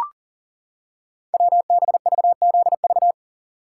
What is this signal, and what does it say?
Morse code, heard as a keyed tone, about 39 wpm, 690 Hz. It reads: WBVZV
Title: Code Groups - 5 Letters - 40wpm